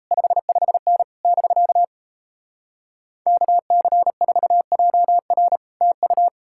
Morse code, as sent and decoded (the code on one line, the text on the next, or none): ..... ..... -. / -...-.- / -.- -.-. ....- .--- .-. / - ..-
55N <BK> KC4JR TU